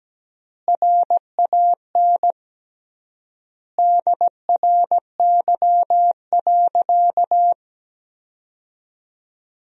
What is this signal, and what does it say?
Morse code, heard as a keyed tone, about 17 wpm, 700 Hz. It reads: RAN DRY.
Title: 3x Word-Spacing - Sentences from Top 500 Words - 17wpm